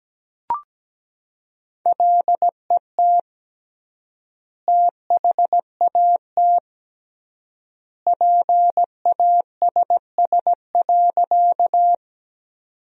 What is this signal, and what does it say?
Morse code, heard as a keyed tone, about 17 wpm, 700 Hz. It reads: LET THAT PASS.
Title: 3x Word-Spacing - Sentences from Top 400 Words - 17wpm